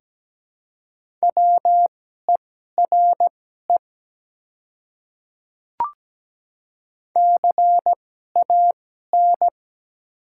Morse code, heard as a keyed tone, about 17 wpm, spaced as for 12 wpm, 690 Hz. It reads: WERE CAN
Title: 2x Character Spacing - Top 100 Words - 17wpm